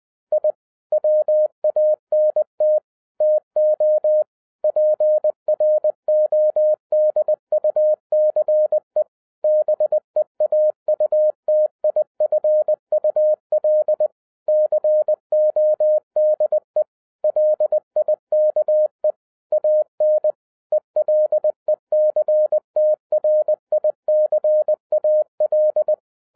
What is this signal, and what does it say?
Morse code, heard as a keyed tone, about 20 wpm, 610 Hz. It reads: I WANT TO PRODUCE BEAUTIFUL CODE LIKE AN ELECTRICAL